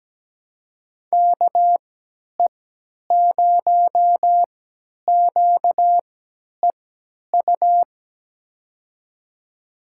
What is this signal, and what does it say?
Morse code, heard as a keyed tone, about 17 wpm, spaced as for 10 wpm, 700 Hz. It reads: KE0QEU
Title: USA Call Signs - 3x Character Spacing - 17wpm